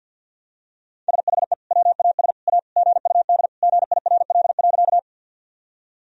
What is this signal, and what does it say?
Morse code, heard as a keyed tone, about 52 wpm, 700 Hz. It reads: SHE WAS A GUD GIRL.
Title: Sentences from Top 400 Words and Abbreviations 50wpm